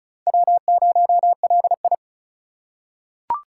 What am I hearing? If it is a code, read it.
Morse code, heard as a keyed tone, about 36 wpm, 700 Hz. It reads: W0LI
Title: USA Call Signs - 4 Characters - 3x Character Spacing - 35wpm